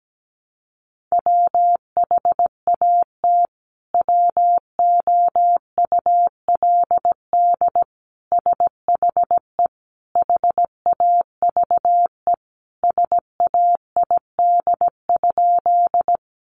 Morse code, heard as a keyed tone, about 17 wpm, 700 Hz. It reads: WHAT WOULD SHE HAVE SAID?
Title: Sentences from Top 100 Words 17wpm